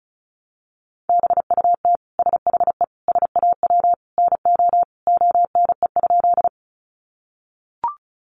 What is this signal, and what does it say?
Morse code, heard as a keyed tone, about 34 wpm, 700 Hz. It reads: BUT SHE SAW NO ONE?
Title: Sentences from Top 300 Words 35wpm